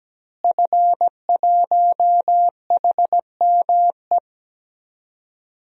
Morse code, heard as a keyed tone, about 17 wpm, 700 Hz. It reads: F1HME
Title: DXCC-100 Call Signs - 5 Characters - Repeated 3x - 17wpm